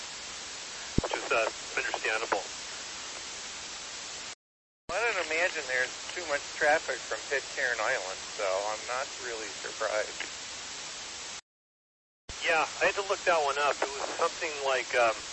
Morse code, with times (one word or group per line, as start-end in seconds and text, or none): none